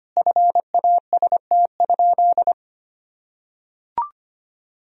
Morse code, heard as a keyed tone, about 25 wpm, 700 Hz. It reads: FAST?
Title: Sentences from Top 1000 Words - Review Part 2 - 25wpm